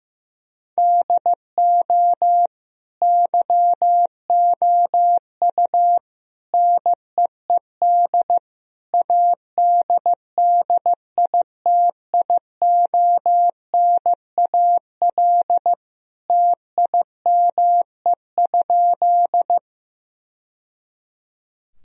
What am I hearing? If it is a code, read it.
Morse code, heard as a keyed tone, about 16 wpm, 700 Hz. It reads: DO YOU NEED ADDITIONAL TIME?
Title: Sentences from Top 1000 Words - Review Part 4 - 15wpm